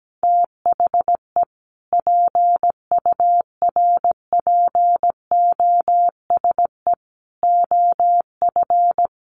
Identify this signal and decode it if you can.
Morse code, heard as a keyed tone, about 17 wpm, 700 Hz. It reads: THE PURPOSE OF
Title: Sentences from Top 800 Words 17wpm